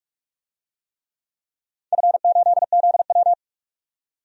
Morse code, heard as a keyed tone, about 46 wpm, 700 Hz. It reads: F8ZW
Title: DXCC-100 Call Signs - 4 Characters - 45wpm